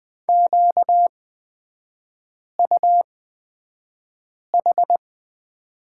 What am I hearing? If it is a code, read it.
Morse code, heard as a keyed tone, about 20 wpm, spaced as for 5 wpm, 700 Hz. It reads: QUH
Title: 250 Lesson 40e - BK - Three Characters 20wpm